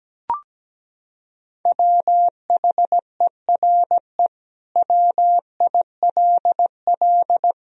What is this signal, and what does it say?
Morse code, heard as a keyed tone, about 17 wpm, 700 Hz. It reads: WHERE WILL
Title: Sentences from Top 1000 Words - Review Part 4 - 17wpm